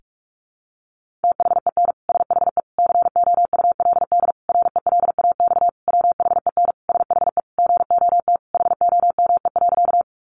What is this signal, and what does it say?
Morse code, heard as a keyed tone, about 44 wpm, 700 Hz. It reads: THEN SHE COULD RELAX WHEN SHE GOT HOME.
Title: Sentences from Top 1400 Words 45wpm